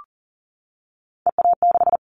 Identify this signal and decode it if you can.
Morse code, heard as a keyed tone, about 40 wpm, 700 Hz. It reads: EA6
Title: Callsign Prefixes for Top 100 DXCC - 40wpm